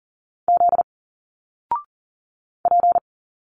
Morse code, heard as a keyed tone, about 40 wpm, 700 Hz. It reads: Z P
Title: Single Letters 40wpm